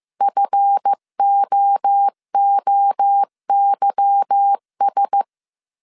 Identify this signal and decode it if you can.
Morse code, heard as a keyed tone, about 15 wpm, 800 Hz. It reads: FOOYS
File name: CW / DET / 1 9 1500C S A 14wpm 3222